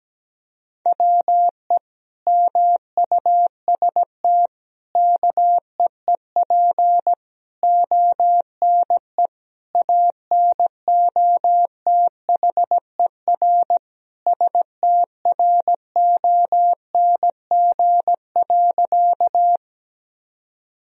Morse code, heard as a keyed tone, about 17 wpm, 700 Hz. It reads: WE MUST KEEP ONE ANOTHER STRONG.
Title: Sentences from Top 1000 Words - Review Part 3 - 17wpm